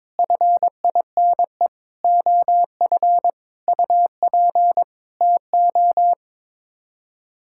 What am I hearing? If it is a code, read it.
Morse code, heard as a keyed tone, about 22 wpm, 700 Hz. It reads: FINE OF UP TO
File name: Sets of 9 Words - 22wpm